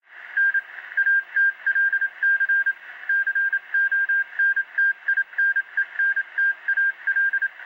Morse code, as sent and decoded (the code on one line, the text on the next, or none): -. / .- - .-.. -.-. / -.-. --- -. - .. -. -. - .- .-..
N ATLC CONTINENTAL